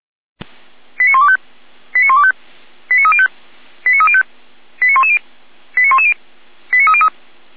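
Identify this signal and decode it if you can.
Five-tone selective call (ZVEI), heard as quick runs of five tones, tones of about 80 ms. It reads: 89125 89125 89305 89305 89110 89110 89303
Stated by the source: Five-tone ZVEI selective call received on SDR.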